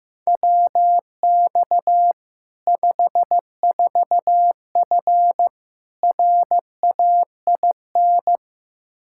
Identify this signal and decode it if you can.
Morse code, heard as a keyed tone, about 15 wpm, 700 Hz. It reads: WX 54F RAIN
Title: Practice QSO - Exchange 2 - Info - 15wpm